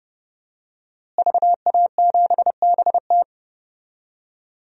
Morse code, heard as a keyed tone, about 30 wpm, 700 Hz. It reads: VA7BT
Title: DXCC-100 Call Signs - 5 Characters - Repeated 2x - 30wpm